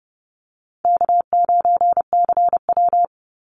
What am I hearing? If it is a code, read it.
Morse code, heard as a keyed tone, about 30 wpm, 700 Hz. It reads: K9CW
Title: Easy Call Signs - 30wpm